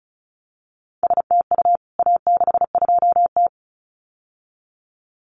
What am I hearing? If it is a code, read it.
Morse code, heard as a keyed tone, about 34 wpm, 700 Hz. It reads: STU A62T
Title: CWT Exchange 35wpm